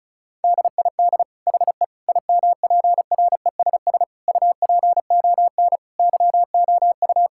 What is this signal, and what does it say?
Morse code, heard as a keyed tone, about 36 wpm, 700 Hz. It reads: DID HE IMPRESS UPON YOU
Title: Sentences from Top 1600 Words 35wpm